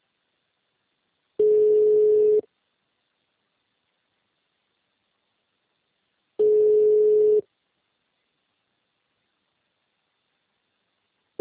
Ringback tone (Europe, 425 Hz).